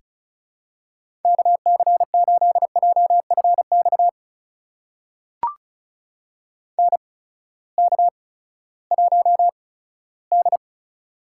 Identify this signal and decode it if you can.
Morse code, heard as a keyed tone, about 36 wpm, spaced as for 18 wpm, 700 Hz. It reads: KC8JFX NK1D
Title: USA Call Signs - 8x Character Spacing - 35wpm